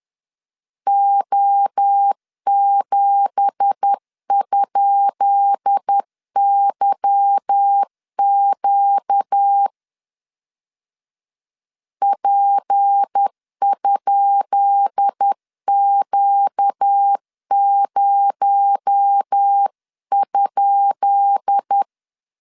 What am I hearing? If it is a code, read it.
Morse code, heard as a keyed tone, about 10 wpm, 800 Hz. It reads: O7?YQ P?Q0?
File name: LEZIONI / 10 SIGNS / 10B INT 10wpm 2844